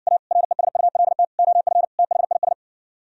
Morse code, heard as a keyed tone, about 59 wpm, 700 Hz. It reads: A RESULT OF THIS